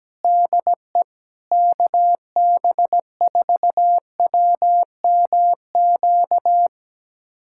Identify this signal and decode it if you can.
Morse code, heard as a keyed tone, about 17 wpm, 700 Hz. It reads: DE KB4WMQ
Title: USA Callsign DE Callsign - 6 Characters - 17wpm